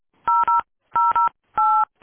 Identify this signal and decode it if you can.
DTMF, heard as telephone touch tones, tones of about 350 ms. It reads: 008